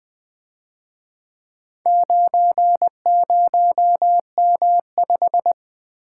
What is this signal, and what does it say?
Morse code, heard as a keyed tone, about 24 wpm, 700 Hz. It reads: 90M5
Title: Speed-Racing - Four Letter-Number Combinations - 20wpm